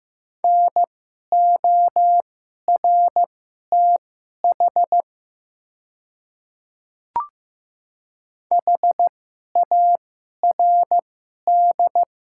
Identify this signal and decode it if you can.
Morse code, heard as a keyed tone, about 15 wpm, spaced as for 11 wpm, 700 Hz. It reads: NORTH HARD
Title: 2x Character Spacing - Top 300 Words - 15wpm